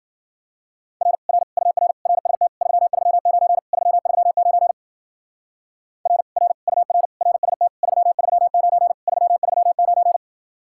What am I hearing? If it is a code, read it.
Morse code, heard as a keyed tone, about 52 wpm, 700 Hz. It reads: R R UR RST 339 339 R R UR RST 339 339